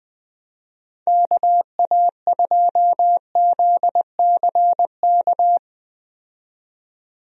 Morse code, heard as a keyed tone, about 20 wpm, 700 Hz. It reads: KA2ZCK
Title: USA Call Signs - 6 Characters - 20wpm